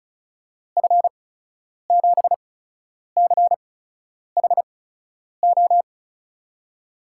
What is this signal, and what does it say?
Morse code, heard as a keyed tone, about 36 wpm, 700 Hz. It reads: F 7 C H O
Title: USA Call Signs - 6 Characters - 8x Character Spacing - 35wpm